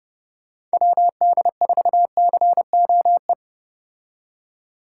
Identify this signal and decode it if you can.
Morse code, heard as a keyed tone, about 31 wpm, 700 Hz. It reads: WD4COE
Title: Speed Racing - USA Call Signs 25wpm